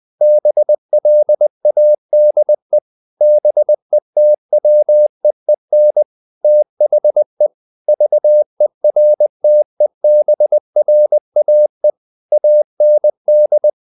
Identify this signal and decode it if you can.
Morse code, heard as a keyed tone, about 20 wpm, 600 Hz. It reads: BLADE BETWEEN THE VERTEBRAE AND
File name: the old man and the sea part 2 20wpm